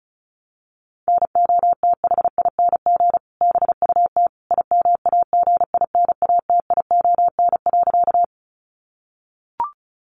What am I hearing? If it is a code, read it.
Morse code, heard as a keyed tone, about 34 wpm, 700 Hz. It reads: NOTHING BUT IMAGINATION.
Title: Sentences from Top 1300 Words 35wpm